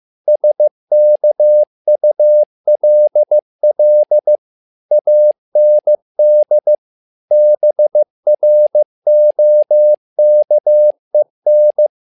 Morse code, heard as a keyed tone, about 15 wpm, 600 Hz. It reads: SKULL AND BROKEN
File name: learning center / tower trouble / tower trouble 15wpm